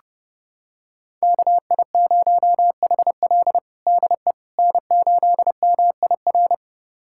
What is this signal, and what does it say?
Morse code, heard as a keyed tone, about 30 wpm, 700 Hz. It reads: KI0HL DE N8MIR
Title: USA Callsign DE Callsign - 5 Characters - 30wpm